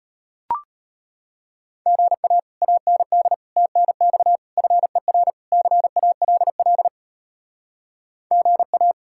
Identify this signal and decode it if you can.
Morse code, heard as a keyed tone, about 33 wpm, 700 Hz. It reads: GA AND TNX FER CALL GA